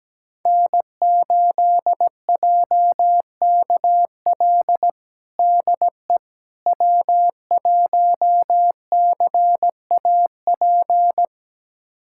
Morse code, heard as a keyed tone, about 17 wpm, 700 Hz. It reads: N8JKL DE W1CAP
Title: USA Callsign DE Callsign - 5 Characters - 17wpm